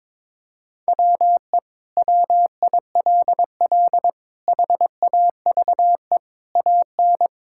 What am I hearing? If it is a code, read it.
Morse code, heard as a keyed tone, about 22 wpm, 700 Hz. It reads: WE WILL HAVE AN
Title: Sentences from Top 1000 Words - Review Part 5 - 22wpm